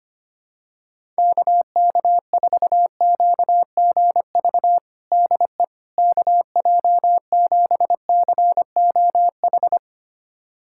Morse code, heard as a keyed tone, about 25 wpm, 700 Hz. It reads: KK4QGV DE KJ7COH
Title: USA Callsign DE Callsign 25wpm